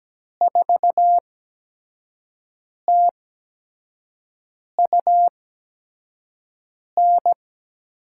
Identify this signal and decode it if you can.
Morse code, heard as a keyed tone, about 17 wpm, spaced as for 5 wpm, 700 Hz. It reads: 4TUN
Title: USA Call Signs - 6 Characters - 8x Character Spacing - 17wpm